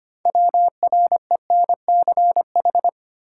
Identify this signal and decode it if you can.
Morse code, heard as a keyed tone, about 25 wpm, 700 Hz. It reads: WRENCH
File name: practice / 25wpm / Easy Words with Hints - 25wpm